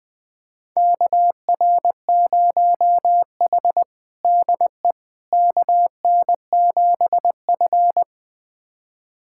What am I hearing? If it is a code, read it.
Morse code, heard as a keyed tone, about 20 wpm, 700 Hz. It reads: KR0H DE KN7F
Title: USA Callsign DE Callsign - 4 Characters - 20wpm